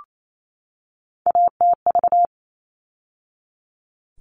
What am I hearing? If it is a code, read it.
Morse code, heard as a keyed tone, about 28 wpm, 700 Hz. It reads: ATV
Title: All Cut Numbers - 3-Digit 28wpm